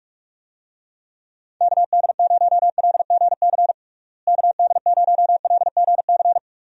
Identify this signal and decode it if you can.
Morse code, heard as a keyed tone, about 47 wpm, 700 Hz. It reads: KD0LGC KD0LGC
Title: USA Call Signs - 6 Characters - Repeated 2x - 45wpm